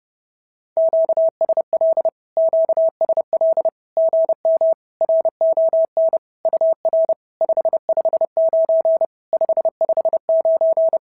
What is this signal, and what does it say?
Morse code, heard as a keyed tone, about 30 wpm, 690 Hz. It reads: QSL QSL GM RON UR 559 559
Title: SOTA QSO Practice 30wpm